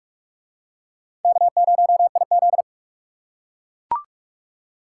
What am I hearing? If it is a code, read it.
Morse code, heard as a keyed tone, about 42 wpm, 700 Hz. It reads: K0IZ